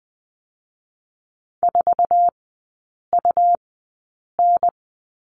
Morse code, heard as a keyed tone, about 20 wpm, spaced as for 8 wpm, 700 Hz. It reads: 4UN
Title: 121 Lesson 20e - Number 3 - Three Characters 20wpm